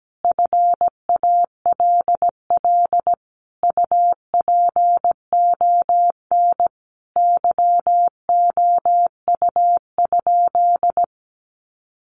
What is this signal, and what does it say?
Morse code, heard as a keyed tone, about 17 wpm, 700 Hz. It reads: FALL UPON YOU?
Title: Sentences from Top 1100 Words 17wpm